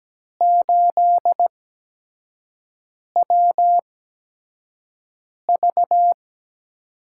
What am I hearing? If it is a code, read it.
Morse code, heard as a keyed tone, about 17 wpm, spaced as for 5 wpm, 700 Hz. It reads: 8WV
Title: USA Call Signs - 4 Characters - 8x Character Spacing - 17wpm